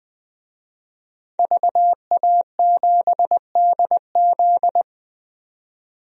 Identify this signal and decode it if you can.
Morse code, heard as a keyed tone, about 20 wpm, 700 Hz. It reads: VA7DZ
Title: DXCC-100 Call Signs - 5 Characters - 20wpm